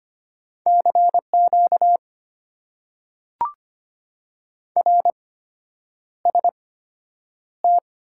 Morse code, heard as a keyed tone, about 25 wpm, spaced as for 9 wpm, 690 Hz. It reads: CQ RST
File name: Top 90 QSO Elements - 8x Character Spacing - 25wpm